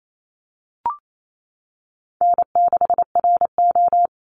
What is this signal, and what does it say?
Morse code, heard as a keyed tone, about 28 wpm, 700 Hz. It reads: N6RO
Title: Call Signs Repeated 3x - 28wpm